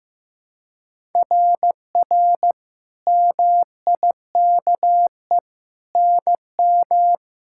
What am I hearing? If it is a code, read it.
Morse code, heard as a keyed tone, about 15 wpm, 700 Hz. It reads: RR MIKE NM
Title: SOTA QSO Practice 15wpm